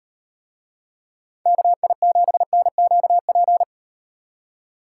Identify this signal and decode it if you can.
Morse code, heard as a keyed tone, about 38 wpm, 700 Hz. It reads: KI7NQP